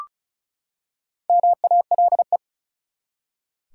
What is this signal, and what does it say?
Morse code, heard as a keyed tone, about 36 wpm, 700 Hz. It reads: MALE
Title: Four Letter Words - 35wpm